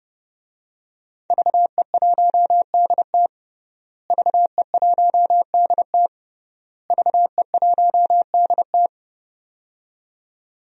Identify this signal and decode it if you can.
Morse code, heard as a keyed tone, about 30 wpm, 700 Hz. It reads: VE1DT VE1DT VE1DT
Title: DXCC-100 Call Signs - 5 Characters - Repeated 3x - 30wpm